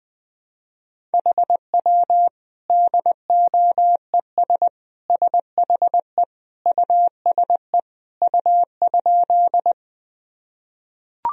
Morse code, heard as a keyed tone, about 21 wpm, 700 Hz. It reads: HW DOES SHE USE U?
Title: Sentences from Top 200 Words and Abbreviations 20wpm